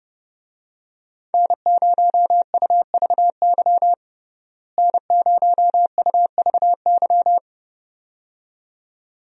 Morse code, heard as a keyed tone, about 30 wpm, 700 Hz. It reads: N0UVY N0UVY
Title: USA Call Signs - 5 Characters - Repeated 2x - 30wpm